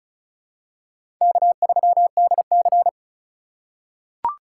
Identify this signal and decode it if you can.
Morse code, heard as a keyed tone, about 36 wpm, 700 Hz. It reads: K3DC
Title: USA Call Signs - 4 Characters - 4x Character Spacing - 35wpm